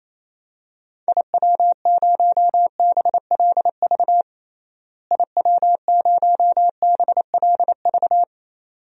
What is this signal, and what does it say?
Morse code, heard as a keyed tone, about 27 wpm, 700 Hz. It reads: IW0BLV IW0BLV
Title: DXCC-100 Call Signs - 6 Characters - Repeated 2x - 28wpm